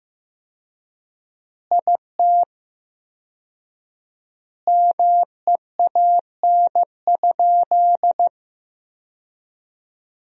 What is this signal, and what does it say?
Morse code, heard as a keyed tone, about 15 wpm, 700 Hz. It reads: IT MEAN?
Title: 4x Word-Spacing - Sentences from Top 200 Words - 15wpm